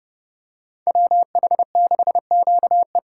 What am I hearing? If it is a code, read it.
Morse code, heard as a keyed tone, about 29 wpm, 700 Hz. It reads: WH6QE